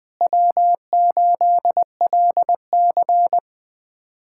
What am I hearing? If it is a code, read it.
Morse code, heard as a keyed tone, about 20 wpm, 700 Hz. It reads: W8LC
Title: USA Callsign DE Callsign - 4 Characters - 20wpm